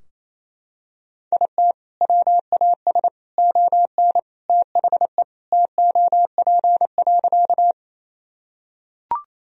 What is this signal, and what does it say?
Morse code, heard as a keyed tone, about 28 wpm, 700 Hz. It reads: IT WAS ON THE TOP.
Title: Sentences from Top 1000 Words - Review Part 2 - 28wpm